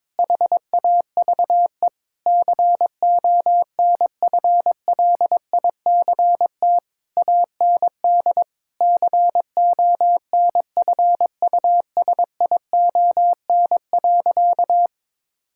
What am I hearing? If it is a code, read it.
Morse code, heard as a keyed tone, about 22 wpm, 700 Hz. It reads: HAVE CONFLICT AND CONFUSION.
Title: Sentences from Top 1600 Words 22wpm